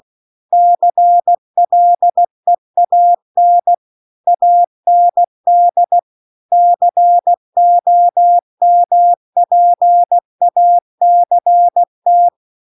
Morse code, heard as a keyed tone, about 16 wpm, 700 Hz. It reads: CLEAN AND COMPACT
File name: CW-The-Big-Twoheaded-River-2-16WPM